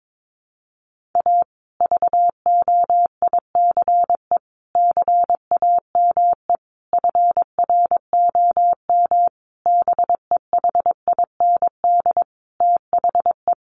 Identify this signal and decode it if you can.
Morse code, heard as a keyed tone, about 22 wpm, 700 Hz. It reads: A VOICE CAME FROM BEHIND THE
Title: Sentences from Top 1400 Words 22wpm